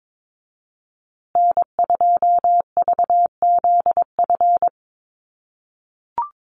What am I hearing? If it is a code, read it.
Morse code, heard as a keyed tone, about 22 wpm, 700 Hz. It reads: N2VZF
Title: USA Call Signs Repeated 3x 22wpm